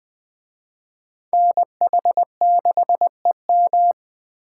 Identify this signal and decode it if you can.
Morse code, heard as a keyed tone, about 20 wpm, 700 Hz. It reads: NH6EM